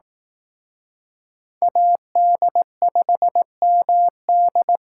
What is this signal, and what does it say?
Morse code, heard as a keyed tone, about 18 wpm, 700 Hz. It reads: AD5MD
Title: Speed Racing - USA 2x2 Call Signs 15wpm